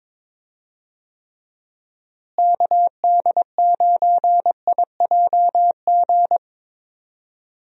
Morse code, heard as a keyed tone, about 22 wpm, 700 Hz. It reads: KD9IJG
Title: USA Call Signs - 6 Characters - Repeated 3x - 22wpm